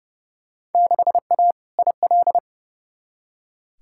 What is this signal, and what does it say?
Morse code, heard as a keyed tone, about 30 wpm, 700 Hz. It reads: 6A IL
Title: Field Day Exchanges - 30wpm